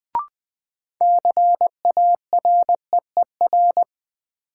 Morse code, heard as a keyed tone, about 20 wpm, 700 Hz. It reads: CAREER
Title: Sentences from Top 1000 Words - New Word Drill - 20wpm